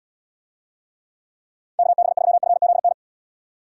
Morse code, heard as a keyed tone, about 72 wpm, 700 Hz. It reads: KB3FZN